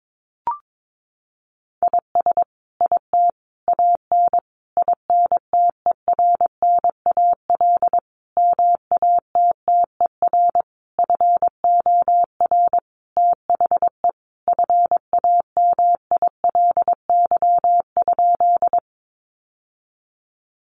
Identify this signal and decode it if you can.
Morse code, heard as a keyed tone, about 22 wpm, 700 Hz. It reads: IS IT AN INTERNAL MATTER FOR THE FAMILY?
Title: Sentences from Top 1400 Words 22wpm